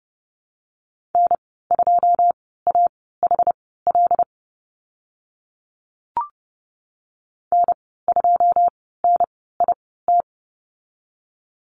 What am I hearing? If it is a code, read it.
Morse code, heard as a keyed tone, about 30 wpm, spaced as for 17 wpm, 700 Hz. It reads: N2AHL N2NIT
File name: Lesson 14g - Number 2 - Call Signs 30wpm